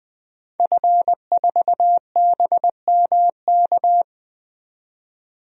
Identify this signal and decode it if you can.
Morse code, heard as a keyed tone, about 20 wpm, 700 Hz. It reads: F4BMK
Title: DXCC-100 Call Signs - 5 Characters - 20wpm